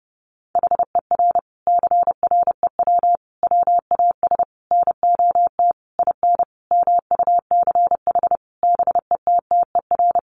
Morse code, heard as a keyed tone, about 30 wpm, 700 Hz. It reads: HER CREW WAS NOT IN MUCH BETTER
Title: Sentences from Top 1500 Words 30wpm